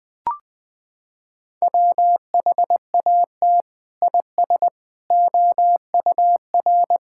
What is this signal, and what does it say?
Morse code, heard as a keyed tone, about 20 wpm, 700 Hz. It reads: WHAT IS OUR
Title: Sentences from Top 1000 Words - Review Part 5 - 20wpm